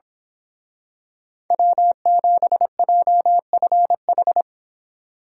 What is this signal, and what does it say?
Morse code, heard as a keyed tone, about 26 wpm, 700 Hz. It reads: W7JFH
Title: Speed Racing - USA 1x3 Call Signs 22wpm